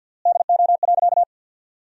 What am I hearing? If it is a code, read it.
Morse code, heard as a keyed tone, about 50 wpm, 700 Hz. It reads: DO.
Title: 3x Word-Spacing - Sentences from Top 100 Words - 50wpm